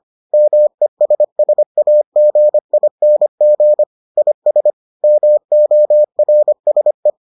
Morse code, heard as a keyed tone, about 25 wpm, 600 Hz. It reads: MESSAGING IS MORSE